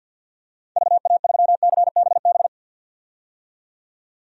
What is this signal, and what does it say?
Morse code, heard as a keyed tone, about 52 wpm, 700 Hz. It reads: VA3CBB
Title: DXCC-100 Call Signs - 6 Characters - 50wpm